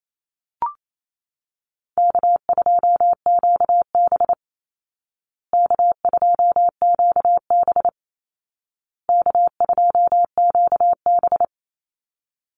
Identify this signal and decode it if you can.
Morse code, heard as a keyed tone, about 28 wpm, 700 Hz. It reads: K2QB K2QB K2QB
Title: Easy Call Signs Repeated 3x - 28wpm